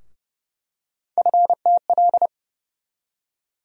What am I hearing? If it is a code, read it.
Morse code, heard as a keyed tone, about 30 wpm, 700 Hz. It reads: FTL
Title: USA Call Sign Postfixes - 3 Letters 30wpm